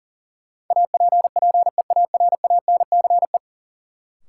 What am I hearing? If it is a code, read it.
Morse code, heard as a keyed tone, about 39 wpm, 700 Hz. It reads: APPEARANCE
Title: Sentences from Top 1100 Words - New Word Drill - 40wpm